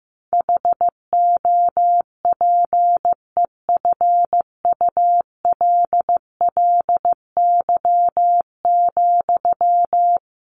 Morse code, heard as a keyed tone, about 15 wpm, 700 Hz. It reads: HOPEFULLY,
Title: Sentences from Top 1500 Words 15wpm